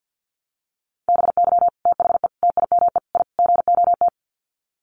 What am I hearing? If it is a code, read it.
Morse code, heard as a keyed tone, about 50 wpm, 700 Hz. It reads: BY THE TIME I GOT
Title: Sets of 5 Words Encore2 50wpm